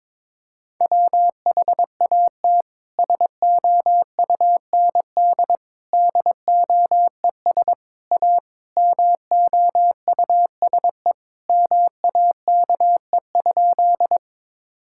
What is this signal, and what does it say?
Morse code, heard as a keyed tone, about 22 wpm, 700 Hz. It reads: WHAT SOUND DOES A MOUSE MAKE?
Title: Sentences from Top 1000 Words - Review Part 5 - 22wpm